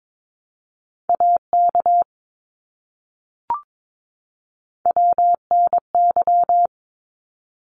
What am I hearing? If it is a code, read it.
Morse code, heard as a keyed tone, about 22 wpm, 700 Hz. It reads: AK WNY
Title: ARRL Sections 22wpm